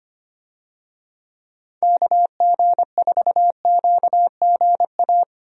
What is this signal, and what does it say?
Morse code, heard as a keyed tone, about 25 wpm, 700 Hz. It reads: KG4QGA
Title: USA Call Signs - 6 Characters - Repeated 3x - 25wpm